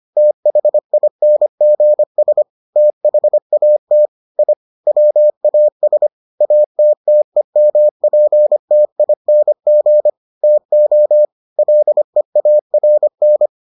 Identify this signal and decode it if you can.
Morse code, heard as a keyed tone, about 25 wpm, 600 Hz. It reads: THINGS THAT I WAS ATTEMPTING TO LEARN